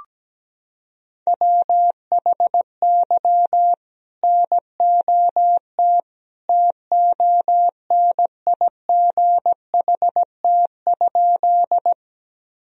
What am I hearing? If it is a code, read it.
Morse code, heard as a keyed tone, about 17 wpm, 700 Hz. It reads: WHY NOT TONIGHT?
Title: Sentences from Top 1000 Words - Review Part 3 - 17wpm